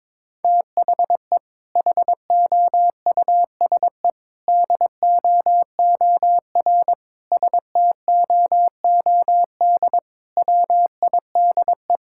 Morse code, heard as a keyed tone, about 21 wpm, 700 Hz. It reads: THE HOUSE DOOR STOOD WIDE
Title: Sentences from Top 1000 Words - Review Part 6 - 22wpm